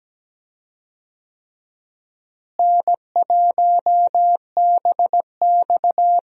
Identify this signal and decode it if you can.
Morse code, heard as a keyed tone, about 17 wpm, 700 Hz. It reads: N1BX
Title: USA Call Signs - 4 Characters - Repeated 3x - 17wpm